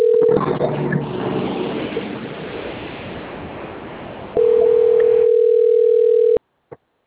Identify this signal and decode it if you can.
Ringback tone (North America), 440 + 480 Hz.